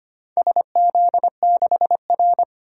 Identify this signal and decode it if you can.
Morse code, heard as a keyed tone, about 25 wpm, 700 Hz. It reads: SZ6R
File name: practice / 25wpm / Four Letter-Number Combinations - 25wpm